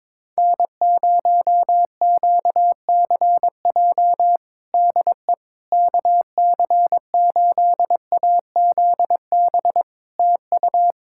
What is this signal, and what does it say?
Morse code, heard as a keyed tone, about 22 wpm, 700 Hz. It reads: N0QCJ DE KC8AZB TU
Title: Practice QSO - Exchange 4 - Ending - 22wpm